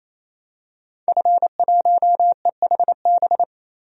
Morse code, heard as a keyed tone, about 28 wpm, 700 Hz. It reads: F1EHB
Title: DXCC-100 Call Signs - 5 Characters - 28wpm